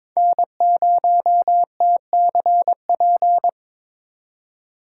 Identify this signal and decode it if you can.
Morse code, heard as a keyed tone, about 26 wpm, 700 Hz. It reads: N0TCP
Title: Speed Racing - USA 1x3 Call Signs 22wpm